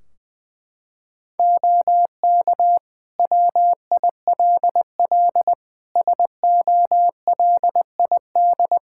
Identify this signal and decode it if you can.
Morse code, heard as a keyed tone, about 21 wpm, 700 Hz. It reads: OK WILL SOLID
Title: Practice QSO - Exchange 3 - More Info - 20wpm